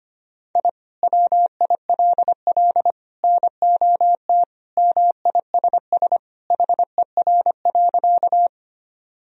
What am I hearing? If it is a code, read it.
Morse code, heard as a keyed tone, about 25 wpm, 700 Hz. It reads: I WILL NOT MISS HER.
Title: Sentences from Top 1000 Words - Review Part 3 - 25wpm